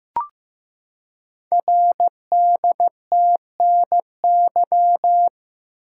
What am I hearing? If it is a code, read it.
Morse code, heard as a keyed tone, about 15 wpm, 700 Hz. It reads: RDTNY
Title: Code Groups - 5 Letters - 15wpm